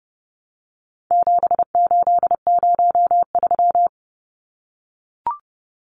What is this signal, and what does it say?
Morse code, heard as a keyed tone, about 30 wpm, 700 Hz. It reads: 7803
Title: Numbers - 4-Digit 30wpm